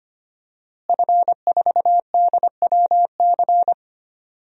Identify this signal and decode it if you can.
Morse code, heard as a keyed tone, about 25 wpm, 700 Hz. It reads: F4DWC
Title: DXCC-100 Call Signs - 5 Characters - Repeated 2x - 25wpm